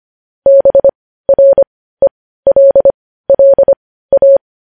Morse code, heard as a keyed tone, about 25 wpm, spaced as for 15 wpm, 550 Hz. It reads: BRELLA
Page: news-headlines-in-morse-code-at-15-wpm